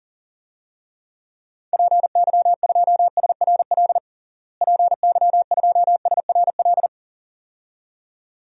Morse code, heard as a keyed tone, about 40 wpm, 700 Hz. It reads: PY2SRL PY2SRL
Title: DXCC-100 Call Signs - 6 Characters - Repeated 2x - 40wpm